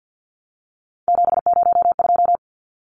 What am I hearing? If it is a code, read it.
Morse code, heard as a keyed tone, about 50 wpm, 700 Hz. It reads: 702
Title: Numbers - 3-Digit 50wpm